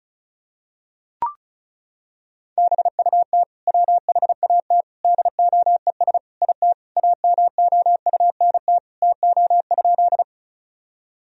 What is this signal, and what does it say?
Morse code, heard as a keyed tone, about 34 wpm, 700 Hz. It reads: BUT WHAT DOES IT AMOUNT TO?
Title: Sentences from Top 1000 Words - Review Part 4 - 35wpm